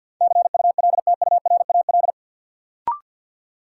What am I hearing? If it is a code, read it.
Morse code, heard as a keyed tone, about 48 wpm, 700 Hz. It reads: CULTURAL
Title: Sentences from Top 1500 Words - New Word Drill - 50wpm